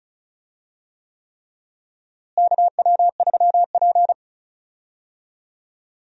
Morse code, heard as a keyed tone, about 36 wpm, 700 Hz. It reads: KW3P
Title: USA Call Signs - 4 Characters - Repeated 3x - 35wpm